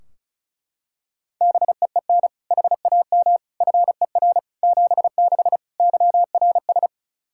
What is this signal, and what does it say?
Morse code, heard as a keyed tone, about 34 wpm, 700 Hz. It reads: BEEN HAM FER 76 YRS
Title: K1USN SST QSO Practice 35wpm